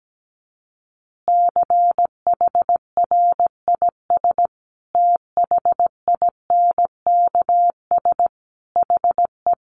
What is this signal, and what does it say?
Morse code, heard as a keyed tone, about 17 wpm, 700 Hz. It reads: CHRIS THINKS HE
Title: Additional Easy Phrases 17wpm